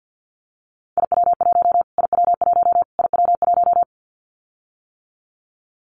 Morse code, heard as a keyed tone, about 50 wpm, 700 Hz. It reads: IW1 IW1 IW1
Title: Callsign Prefixes for Top 25 DXCC - Repeated 3x - 50wpm